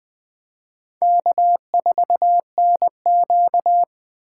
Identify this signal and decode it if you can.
Morse code, heard as a keyed tone, about 24 wpm, 700 Hz. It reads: K4NQ
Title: Speed Racing - USA 1x2 Call Signs 20wpm